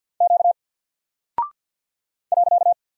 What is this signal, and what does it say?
Morse code, heard as a keyed tone, about 50 wpm, 700 Hz. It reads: Q .
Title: Single Character (Letter:Number:Punctuation:Prosign) 50wpm